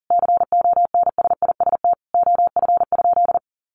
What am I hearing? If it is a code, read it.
Morse code, heard as a keyed tone, about 40 wpm, 700 Hz. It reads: CONSIST OF?
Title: Sentences from Top 1400 Words 40wpm